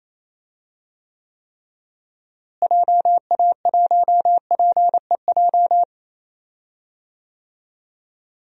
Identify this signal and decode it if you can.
Morse code, heard as a keyed tone, about 28 wpm, 700 Hz. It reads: JA1PEJ
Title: DXCC-100 Call Signs - 6 Characters - Repeated 3x - 28wpm